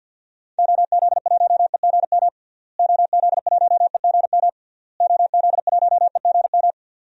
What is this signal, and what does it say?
Morse code, heard as a keyed tone, about 52 wpm, 700 Hz. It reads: OZ1EGM OZ1EGM OZ1EGM
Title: DXCC-100 Call Signs - 6 Characters - Repeated 3x - 50wpm